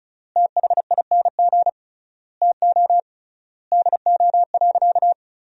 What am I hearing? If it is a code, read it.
Morse code, heard as a keyed tone, about 34 wpm, 700 Hz. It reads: THING TO DO.
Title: 3x Word-Spacing - Sentences from Top 100 Words - 35wpm